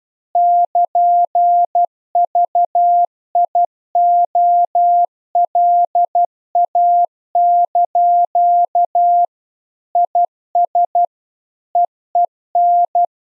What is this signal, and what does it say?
Morse code, heard as a keyed tone, about 12 wpm, 700 Hz. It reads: <KN>VIOLA) IS EEN
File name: Zendamateurnieuws en een stukje geseinde morse